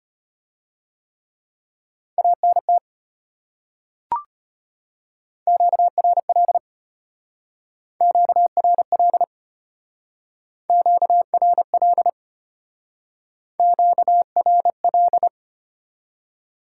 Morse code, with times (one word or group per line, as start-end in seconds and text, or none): 2.2-2.8 ANT
5.5-6.6 QRL
8.0-9.2 QRL
10.7-12.1 QRL
13.6-15.3 QRL